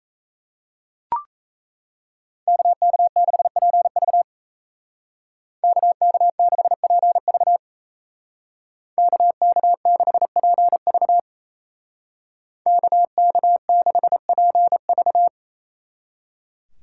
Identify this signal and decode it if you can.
Morse code, heard as a keyed tone, about 33 wpm, 700 Hz. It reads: KK6PV KK6PV KK6PV KK6PV
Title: Speed Racing - USA 2x2 Call Signs 28wpm